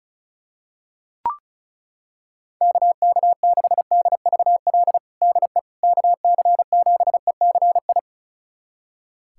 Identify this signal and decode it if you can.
Morse code, heard as a keyed tone, about 36 wpm, 700 Hz. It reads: KK6DVL DE KC7ECI
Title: USA Callsign DE Callsign - 6 Characters - 35wpm